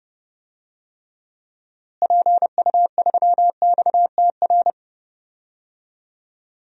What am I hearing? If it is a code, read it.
Morse code, heard as a keyed tone, about 30 wpm, 700 Hz. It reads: PU3XTR
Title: DXCC-100 Call Signs - 6 Characters - Repeated 2x - 30wpm